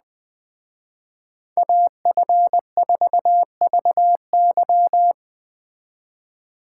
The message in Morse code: AF4VY

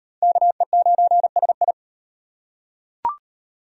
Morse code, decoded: KE9SI

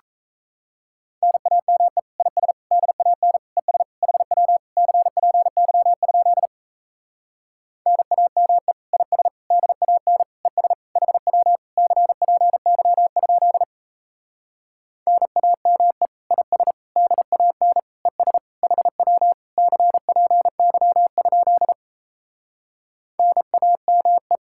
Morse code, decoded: NAME IS DAN ES HW CPY? NAME IS DAN ES HW CPY? NAME IS DAN ES HW CPY? NAME

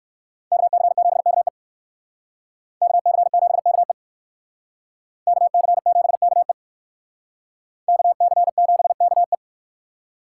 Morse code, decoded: KC7KE KC7KE KC7KE KC7KE

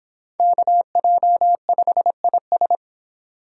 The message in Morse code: KJ5IS

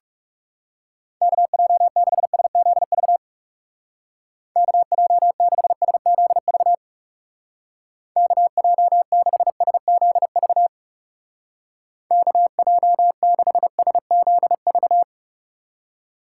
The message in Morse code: KJ6SZV KJ6SZV KJ6SZV KJ6SZV

